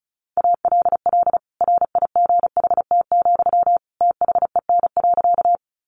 Morse code, heard as a keyed tone, about 34 wpm, 700 Hz. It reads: ALL RIGHT, THEN.
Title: Sentences from Top 200 Words 35wpm